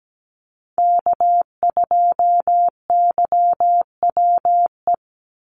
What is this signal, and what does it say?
Morse code, heard as a keyed tone, about 17 wpm, 700 Hz. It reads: K2YWE
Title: Call Signs - 17wpm